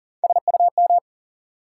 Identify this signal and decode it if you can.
Morse code, heard as a keyed tone, about 38 wpm, 700 Hz. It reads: SUM
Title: Sentences from Top 1000 Words - Review Part 6 - New Word Drill - 40wpm